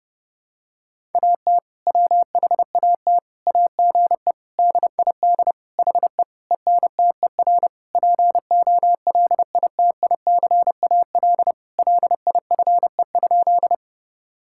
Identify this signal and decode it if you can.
Morse code, heard as a keyed tone, about 32 wpm, 700 Hz. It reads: AT WHAT AGE DID HE ENTER POLITICAL LIFE?
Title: Sentences from Top 1000 Words - Review Part 5 - 30wpm